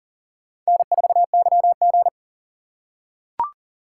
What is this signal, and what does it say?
Morse code, heard as a keyed tone, about 40 wpm, 700 Hz. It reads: N4YG
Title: USA Call Signs - 3x Character Spacing - 40wpm